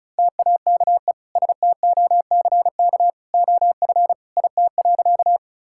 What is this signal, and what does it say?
Morse code, heard as a keyed tone, about 36 wpm, 700 Hz. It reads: TAKE STOCK OF IT.
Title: Sentences from Top 1000 Words - Review Part 5 - 35wpm